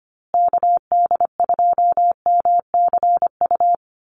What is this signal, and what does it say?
Morse code, heard as a keyed tone, about 25 wpm, 700 Hz. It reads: KD2MCU
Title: USA Call Signs Repeated 2x 25wpm